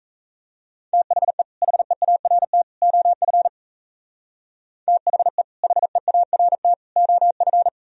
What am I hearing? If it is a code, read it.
Morse code, heard as a keyed tone, about 37 wpm, 700 Hz. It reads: THE HEART OF THE HEART OF